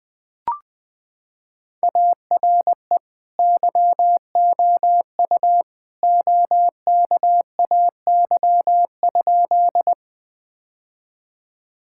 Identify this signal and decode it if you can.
Morse code, heard as a keyed tone, about 20 wpm, 700 Hz. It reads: ARE YOU OKAY?
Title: Sentences with 2-Word Phrases - 20wpm